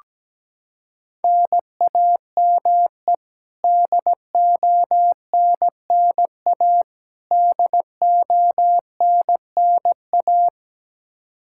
Morse code, heard as a keyed tone, about 17 wpm, 700 Hz. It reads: NAME DONNA DONNA
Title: Practice QSO - Exchange 1 - Beginning - 17wpm